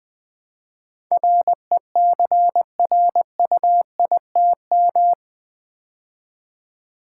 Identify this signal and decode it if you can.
Morse code, heard as a keyed tone, about 20 wpm, 700 Hz. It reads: RECRUITM
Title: Build A Word - 4 Letters At A Time - 11-Letter Words - 20wpm